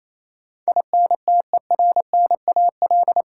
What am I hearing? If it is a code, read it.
Morse code, heard as a keyed tone, about 28 wpm, 700 Hz. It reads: INTERNAL
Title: Sentences from Top 1400 Words - New Word Drill - 28wpm